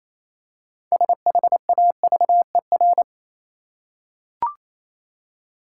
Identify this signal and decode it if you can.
Morse code, heard as a keyed tone, about 27 wpm, 690 Hz. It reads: SHAVER
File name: Easy Words - Speed-Racing with Hints - 22wpm